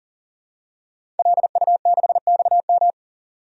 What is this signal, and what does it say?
Morse code, heard as a keyed tone, about 40 wpm, 700 Hz. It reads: LU6XM
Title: DXCC-100 Call Signs - 5 Characters - 40wpm